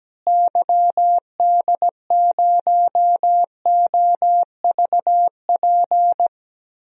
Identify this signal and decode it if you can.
Morse code, heard as a keyed tone, about 17 wpm, 700 Hz. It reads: YD0OVP
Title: DXCC-100 Call Signs - 6 Characters - Repeated 3x - 17wpm